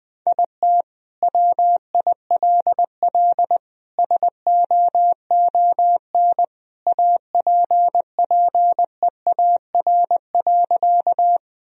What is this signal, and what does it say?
Morse code, heard as a keyed tone, about 20 wpm, 700 Hz. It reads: IT WILL SOON APPEAR.
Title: Sentences from Top 1000 Words - Review Part 2 - 20wpm